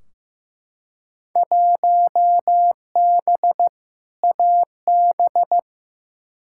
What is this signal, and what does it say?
Morse code, heard as a keyed tone, about 15 wpm, 700 Hz. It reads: 1B AB
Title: Field Day Exchanges - 15wpm